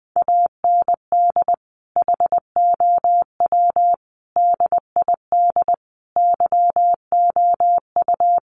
Morse code, heard as a keyed tone, about 20 wpm, 700 Hz. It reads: AND HOW DID YOU
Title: Sentences from Top 200 Words 20wpm